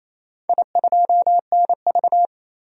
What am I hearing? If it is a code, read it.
Morse code, heard as a keyed tone, about 27 wpm, 700 Hz. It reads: I2NV
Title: DXCC-100 Call Signs - 4 Characters - 28wpm